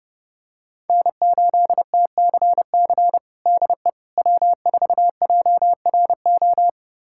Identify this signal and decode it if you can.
Morse code, heard as a keyed tone, about 30 wpm, 700 Hz. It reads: N8TCC DE W4JRO
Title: USA Callsign DE Callsign - 5 Characters - 30wpm